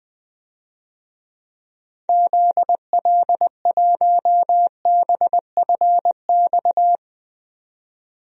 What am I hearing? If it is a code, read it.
Morse code, heard as a keyed tone, about 21 wpm, 700 Hz. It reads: ZL1BFX
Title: DXCC-100 Call Signs - 6 Characters - 20wpm